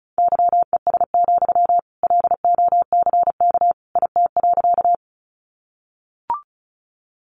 Morse code, heard as a keyed tone, about 34 wpm, 700 Hz. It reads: YES, LOCK IT.
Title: Sentences from Top 1300 Words 35wpm